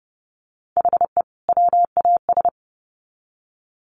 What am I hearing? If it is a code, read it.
Morse code, heard as a keyed tone, about 30 wpm, 700 Hz. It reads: HE WAS
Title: Sets of 2 Words 30wpm